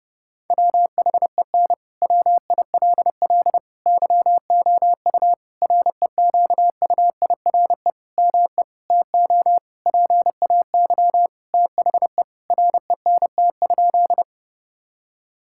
Morse code, heard as a keyed tone, about 30 wpm, 700 Hz. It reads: WHEN WILL YOU REQUIRE ME TO PAY THE RENT?
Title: Sentences from Top 1000 Words - Review Part 6 - 30wpm